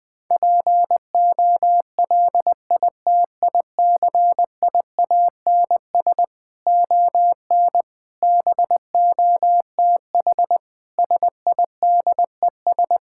Morse code, heard as a keyed tone, about 20 wpm, 700 Hz. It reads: POLITICIANS ON BOTH SIDES
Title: Sentences with 6-Word Phrases - 20wpm